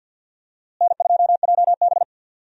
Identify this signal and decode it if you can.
Morse code, heard as a keyed tone, about 50 wpm, 700 Hz. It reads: N2JB
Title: USA Call Signs - 4 Characters - Repeated 2x - 50wpm